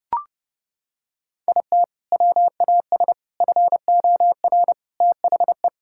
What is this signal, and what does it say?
Morse code, heard as a keyed tone, about 30 wpm, 700 Hz. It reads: IT WAS FOR THE
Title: Sentences from Top 1000 Words - Review Part 6 - 30wpm